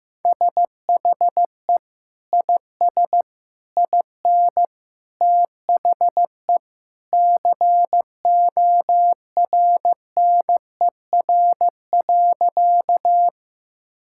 Morse code, heard as a keyed tone, about 15 wpm, 700 Hz. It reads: SHE IS IN THE CORNER.
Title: Sentences from Top 1600 Words 15wpm